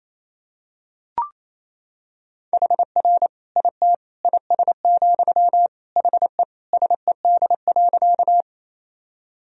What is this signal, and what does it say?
Morse code, heard as a keyed tone, about 28 wpm, 700 Hz. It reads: HR IT IS, HE SED.
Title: Sentences from Top 200 Words and Abbreviations 28wpm